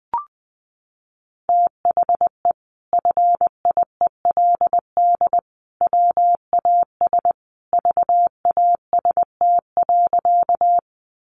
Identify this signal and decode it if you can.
Morse code, heard as a keyed tone, about 20 wpm, 700 Hz. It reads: THE FIELD WAS VAST.
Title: Sentences from Top 1300 Words 20wpm